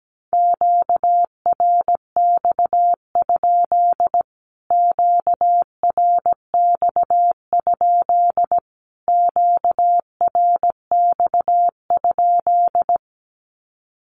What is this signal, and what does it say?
Morse code, heard as a keyed tone, about 17 wpm, 700 Hz. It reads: QRX? QRX? QRX?
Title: Q-Codes Repeated 3x 17wpm